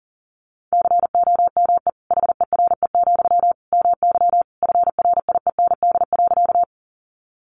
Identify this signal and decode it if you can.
Morse code, heard as a keyed tone, about 40 wpm, 700 Hz. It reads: COME HERE, MY FRIEND.
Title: Sentences from Top 300 Words 40wpm